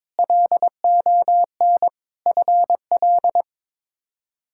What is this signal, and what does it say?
Morse code, heard as a keyed tone, about 21 wpm, 700 Hz. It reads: LON FL
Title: K1USN SST QSO Practice - 22wpm